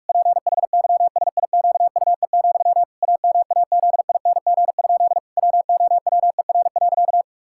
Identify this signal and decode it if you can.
Morse code, heard as a keyed tone, about 46 wpm, 700 Hz. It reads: PHYSIQUE, AMAZING? WOWER.
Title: Pangram Challenge v2 - 45wpm